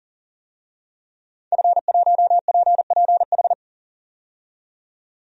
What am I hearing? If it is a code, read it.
Morse code, heard as a keyed tone, about 40 wpm, 700 Hz. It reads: F1PPH
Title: DXCC-100 Call Signs - 5 Characters - 40wpm